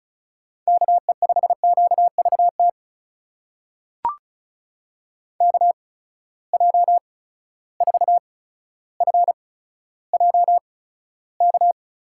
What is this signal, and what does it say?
Morse code, heard as a keyed tone, about 36 wpm, spaced as for 15 wpm, 700 Hz. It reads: KE5QVT KJ4FJK
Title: USA Call Signs - 6 Characters - 8x Character Spacing - 35wpm